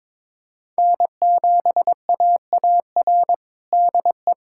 Morse code, heard as a keyed tone, about 22 wpm, 700 Hz. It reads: N7AAR DE
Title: USA Callsign DE Callsign - 5 Characters - 22wpm